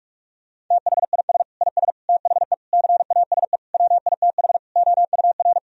Morse code, heard as a keyed tone, about 46 wpm, 700 Hz. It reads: THIS IS THE CASE WITH OUR
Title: Sentences from Top 1000 Words - Review Part 3 - 45wpm